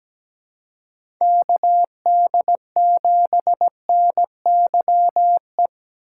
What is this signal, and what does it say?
Morse code, heard as a keyed tone, about 17 wpm, 700 Hz. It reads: K D 7 N Y E
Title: USA Call Signs - 6 Characters - 6x Character Spacing - 17wpm